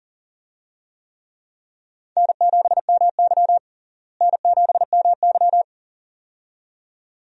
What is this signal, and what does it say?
Morse code, heard as a keyed tone, about 40 wpm, 700 Hz. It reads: N7MY N7MY
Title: USA Call Signs - 4 Characters - Repeated 2x - 40wpm